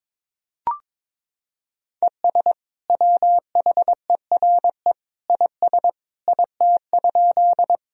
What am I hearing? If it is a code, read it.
Morse code, heard as a keyed tone, about 22 wpm, 700 Hz. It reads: ES WHERE IS IT?
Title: Sentences from Top 200 Words and Abbreviations 22wpm